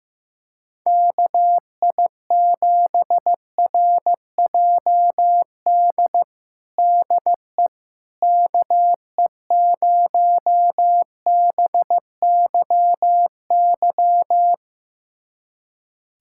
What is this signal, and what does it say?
Morse code, heard as a keyed tone, about 16 wpm, 700 Hz. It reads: KI7RJD DE KE0BYY
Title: USA Callsign DE Callsign - 6 Characters - 15wpm